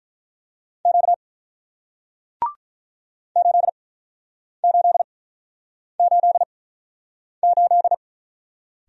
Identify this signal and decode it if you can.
Morse code, heard as a keyed tone, about 47 wpm, spaced as for 9 wpm, 700 Hz. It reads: Q 8888